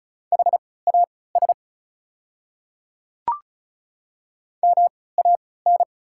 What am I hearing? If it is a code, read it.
Morse code, heard as a keyed tone, about 35 wpm, spaced as for 20 wpm, 700 Hz. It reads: HAS MAN
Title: 3x Character Spacing - Top 200 Words - 35wpm